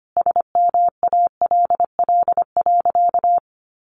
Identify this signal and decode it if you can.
Morse code, heard as a keyed tone, about 25 wpm, 700 Hz. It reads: SMALL.
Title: Additional Easy Phrases 25wpm